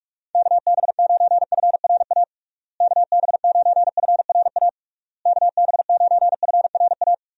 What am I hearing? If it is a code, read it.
Morse code, heard as a keyed tone, about 41 wpm, 700 Hz. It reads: KB9FRA KB9FRA KB9FRA
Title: USA Call Signs - 6 Characters - Repeated 3x - 45wpm